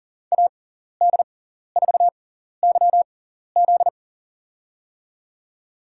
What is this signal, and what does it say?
Morse code, heard as a keyed tone, about 40 wpm, spaced as for 14 wpm, 700 Hz. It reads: AD4YZ
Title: USA Call Signs - 6x Character Spacing - 40wpm